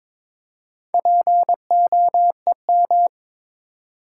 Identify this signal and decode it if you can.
Morse code, heard as a keyed tone, about 22 wpm, 700 Hz. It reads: POEM
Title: Four Letter Words - 22wpm